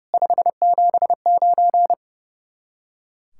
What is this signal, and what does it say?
Morse code, heard as a keyed tone, about 35 wpm, 700 Hz. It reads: 579
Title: Speed-Racing Common QSO Phrases - 30wpm